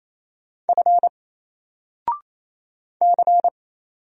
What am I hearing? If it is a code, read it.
Morse code, heard as a keyed tone, about 27 wpm, 700 Hz. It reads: F C
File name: Single-Letter-Number-28wpm